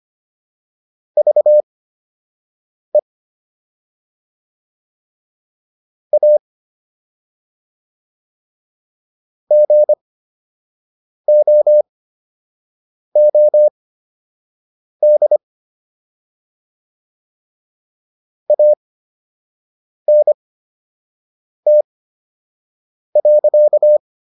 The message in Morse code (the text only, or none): VE A GOOD ANT.